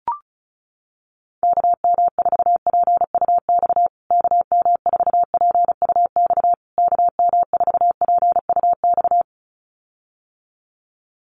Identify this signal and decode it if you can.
Morse code, heard as a keyed tone, about 34 wpm, 700 Hz. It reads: KM4PUX KM4PUX KM4PUX
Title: USA Call Signs Repeated 3x 35wpm